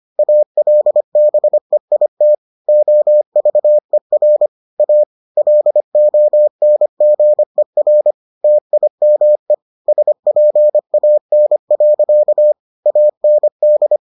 Morse code, heard as a keyed tone, about 25 wpm, 600 Hz. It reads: ALBEIT OVER A LONGER TIME SPAN. AND